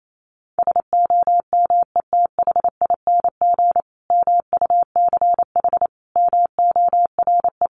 Morse code, heard as a keyed tone, about 28 wpm, 700 Hz. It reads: SOMETHING MUCH MORE
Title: A Sound Of Thunder 28wpm